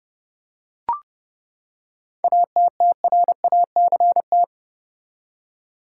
Morse code, heard as a keyed tone, about 30 wpm, 700 Hz. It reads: ATTRACT
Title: Seven Letter Words - 30wpm